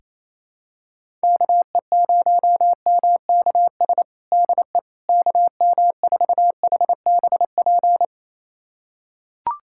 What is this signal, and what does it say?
Morse code, heard as a keyed tone, about 28 wpm, 700 Hz. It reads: KE0MKS DE KM4HBP
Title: USA Callsign DE Callsign - 6 Characters - 28wpm